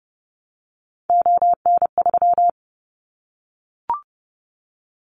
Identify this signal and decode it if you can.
Morse code, heard as a keyed tone, about 30 wpm, 700 Hz. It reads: ON3
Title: Callsign Prefixes for Top 25 DXCC - Repeated 3x - 30wpm